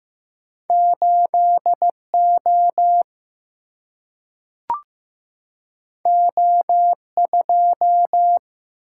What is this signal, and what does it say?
Morse code, heard as a keyed tone, about 15 wpm, 700 Hz. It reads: 8 O O 2
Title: Single-Letter Single-Number 15wpm